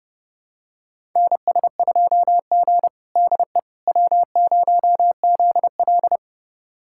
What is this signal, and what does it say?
Morse code, heard as a keyed tone, about 30 wpm, 700 Hz. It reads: NS2G DE W0ZL
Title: USA Callsign DE Callsign - 4 Characters - 30wpm